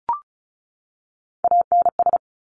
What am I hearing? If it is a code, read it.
Morse code, heard as a keyed tone, about 36 wpm, 700 Hz. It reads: ANS
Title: Top Three Letter Combinations in Words 35wpm